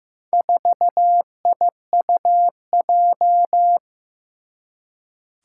Morse code, heard as a keyed tone, about 15 wpm, 700 Hz. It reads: 4IUJ
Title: Four Challenging Letter-Number Combinations 15wpm